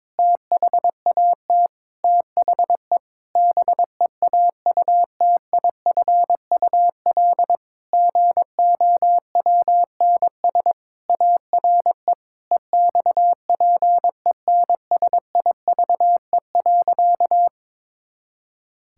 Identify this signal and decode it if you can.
Morse code, heard as a keyed tone, about 22 wpm, 700 Hz. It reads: THAT THE BEAUTIFUL GOWNS ARE EXPENSIVE.
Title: Pangram Challenge v2 - 22wpm